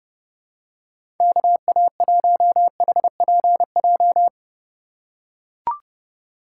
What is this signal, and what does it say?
Morse code, heard as a keyed tone, about 32 wpm, 700 Hz. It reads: KA1HPJ
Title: USA Call Signs - 6 Characters - 2x Character Spacing - 30wpm